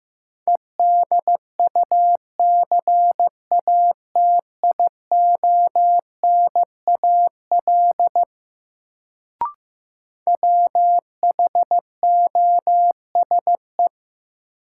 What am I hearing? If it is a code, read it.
Morse code, heard as a keyed tone, about 15 wpm, 700 Hz. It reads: EDUCATIONAL WHOSE
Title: Sentences from Top 1200 Words - New Word Drill - 15wpm